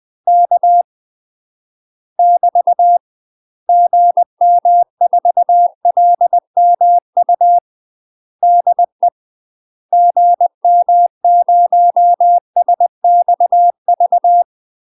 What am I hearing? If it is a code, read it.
Morse code, heard as a keyed tone, about 20 wpm, 700 Hz. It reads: K <BT> GM4LMU DE GM0SXV
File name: UK MASTER QSOs TRAINING/20WPM